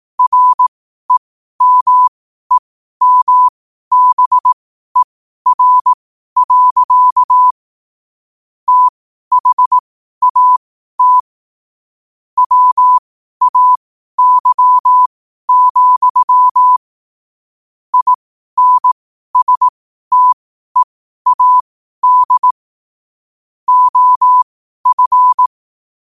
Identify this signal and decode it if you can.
Morse code, heard as a keyed tone, about 18 wpm, spaced as for 12 wpm, 1000 Hz. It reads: REMEMBER. THAT WAY, INSTEAD OF
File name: qotdfiles / 12wpm